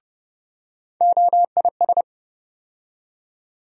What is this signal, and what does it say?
Morse code, heard as a keyed tone, about 30 wpm, 700 Hz. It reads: OIS
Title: USA Call Sign Postfixes - 3 Letters 30wpm